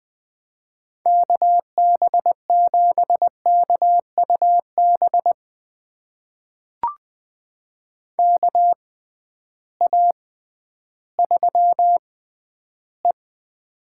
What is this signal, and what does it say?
Morse code, heard as a keyed tone, about 20 wpm, spaced as for 12 wpm, 700 Hz. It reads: KB7KUB KA3E
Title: USA Call Signs - 6x Character Spacing - 20wpm